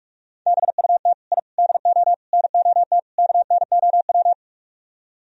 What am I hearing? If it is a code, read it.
Morse code, heard as a keyed tone, about 47 wpm, 700 Hz. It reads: BUT I DO NOT KNOW